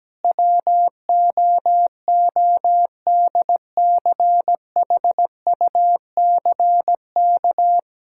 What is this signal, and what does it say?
Morse code, heard as a keyed tone, about 17 wpm, 700 Hz. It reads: WOODCHUCK
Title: Pangram Challenge v2 - 17wpm